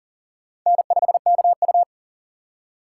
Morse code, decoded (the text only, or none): N5KU